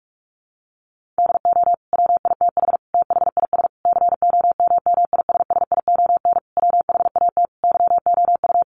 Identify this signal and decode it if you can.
Morse code, heard as a keyed tone, about 44 wpm, 700 Hz. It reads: DO WITH THIS COMMISSION WHAT YOU
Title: Sentences from Top 1400 Words 45wpm